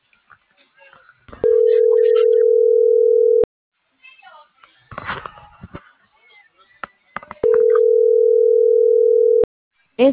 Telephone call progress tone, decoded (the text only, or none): ringback tone